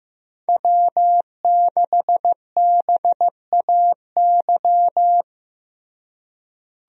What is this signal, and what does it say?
Morse code, heard as a keyed tone, about 15 wpm, 700 Hz. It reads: W6BAY
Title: USA Call Signs - 5 Characters - Repeated 2x - 15wpm